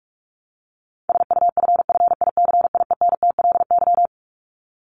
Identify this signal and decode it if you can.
Morse code, heard as a keyed tone, about 44 wpm, 700 Hz. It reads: SUFFICIENTLY
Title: LY Suffix - Top 100 Words with Base Words 45wpm